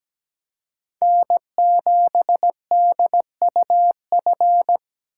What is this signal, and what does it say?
Morse code, heard as a keyed tone, about 20 wpm, 700 Hz. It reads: N7DUF
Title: Speed Racing - USA 1x3 Call Signs 17wpm